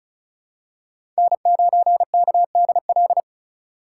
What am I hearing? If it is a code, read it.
Morse code, heard as a keyed tone, about 36 wpm, 700 Hz. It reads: N9KDL
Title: USA Call Signs - 8x Character Spacing - 35wpm